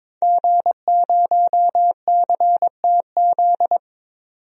Morse code, heard as a keyed tone, about 22 wpm, 700 Hz. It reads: G0CTZ
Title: DXCC-100 Call Signs - 5 Characters - 22wpm